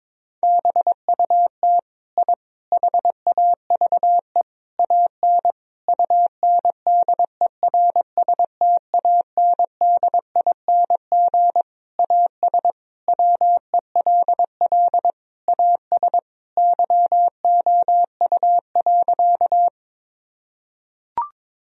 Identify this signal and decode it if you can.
Morse code, heard as a keyed tone, about 22 wpm, 700 Hz. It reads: BUT I HAVE AN UNDERSTANDING AS WELL AS YOU.
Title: Sentences from Top 1000 Words - Review Part 3 - 22wpm